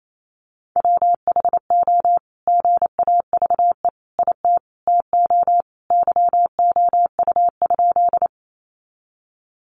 Morse code, heard as a keyed tone, about 28 wpm, 700 Hz. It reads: WHO GAVE IT TO YOU?
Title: Sentences from Top 500 Words 28wpm